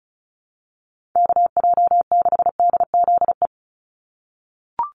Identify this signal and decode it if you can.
Morse code, heard as a keyed tone, about 34 wpm, 700 Hz. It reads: KJ6DZE
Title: USA Call Signs 35wpm